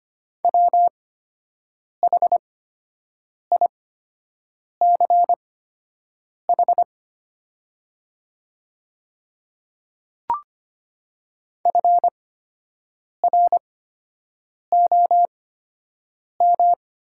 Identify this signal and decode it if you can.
Morse code, heard as a keyed tone, about 25 wpm, spaced as for 7 wpm, 700 Hz. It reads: WHICH FROM
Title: 8x Character Spacing - Top 300 Words - 25wpm